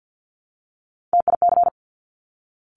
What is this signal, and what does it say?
Morse code, heard as a keyed tone, about 50 wpm, 690 Hz. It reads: TIC
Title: Top Three Letter Combinations in Words 50wpm